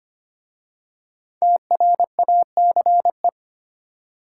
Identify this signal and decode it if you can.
Morse code, heard as a keyed tone, about 25 wpm, 700 Hz. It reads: TRACE
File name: 5 Letter Words - First 3 Letters Given - 25wpm